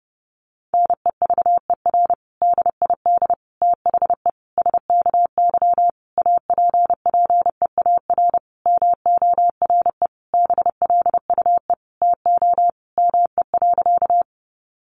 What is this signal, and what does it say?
Morse code, heard as a keyed tone, about 30 wpm, 700 Hz. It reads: NEVER DID THE SKY APPEAR MORE BLUE TO ME.
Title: Sentences from Top 1200 Words 30wpm